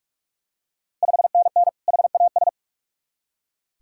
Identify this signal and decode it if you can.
Morse code, heard as a keyed tone, about 46 wpm, 700 Hz. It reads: 5NN HAS
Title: Ontario QSO Party Exchange 45wpm